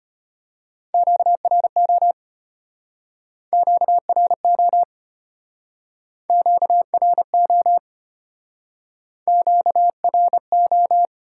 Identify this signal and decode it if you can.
Morse code, heard as a keyed tone, about 33 wpm, 700 Hz. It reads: QRO QRO QRO QRO